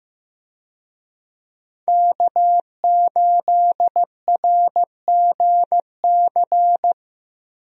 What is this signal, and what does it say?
Morse code, heard as a keyed tone, about 15 wpm, 700 Hz. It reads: K8RGC
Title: USA Call Signs - 5 Characters - Repeated 2x - 15wpm